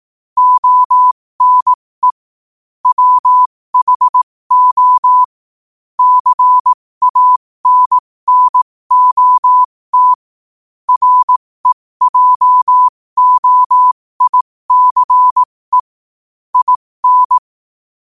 Morse code, heard as a keyed tone, about 18 wpm, spaced as for 15 wpm, 1000 Hz. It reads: ONE WHO CANNOT REJOICE IN
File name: qotdfiles / 15wpm